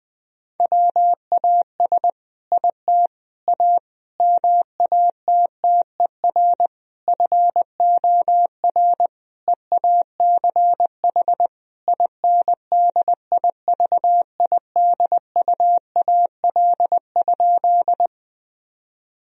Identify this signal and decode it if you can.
Morse code, heard as a keyed tone, about 21 wpm, 700 Hz. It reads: WAS IT A MATTER FOR EACH INDIVIDUAL?
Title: Sentences from Top 1000 Words - Review Part 4 - 20wpm